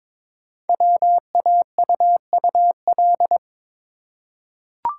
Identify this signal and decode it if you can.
Morse code, heard as a keyed tone, about 22 wpm, 700 Hz. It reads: WAUUL
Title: Code Groups - 5 Letters - 22wpm